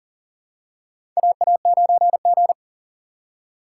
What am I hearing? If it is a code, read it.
Morse code, heard as a keyed tone, about 40 wpm, 700 Hz. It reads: AA9G
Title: USA Call Signs - 4 Characters - 4x Character Spacing - 40wpm